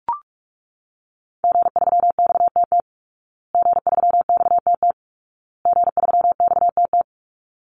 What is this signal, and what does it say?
Morse code, heard as a keyed tone, about 44 wpm, 700 Hz. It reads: G3XTT G3XTT G3XTT
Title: Call Signs Repeated 2x - 45wpm